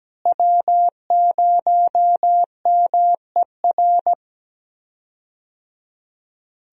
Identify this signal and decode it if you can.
Morse code, heard as a keyed tone, about 17 wpm, 700 Hz. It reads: W0MER
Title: USA Call Signs - 5 Characters - Repeated 3x - 17wpm